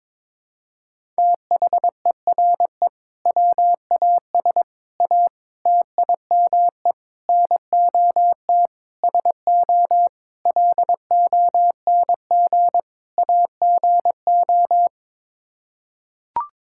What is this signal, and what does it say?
Morse code, heard as a keyed tone, about 21 wpm, 700 Hz. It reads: THERE WAS A TIME NOT SO LONG AGO
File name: practice / 22wpm / Sets of 8 Words - 22wpm